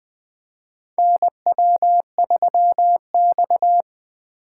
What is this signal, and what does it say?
Morse code, heard as a keyed tone, about 20 wpm, 700 Hz. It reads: NW3X
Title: USA Call Signs - 4 Characters - 4x Character Spacing - 20wpm